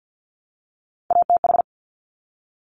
Morse code, heard as a keyed tone, about 50 wpm, 700 Hz. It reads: ATH
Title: Top Three Letter Combinations in Words 50wpm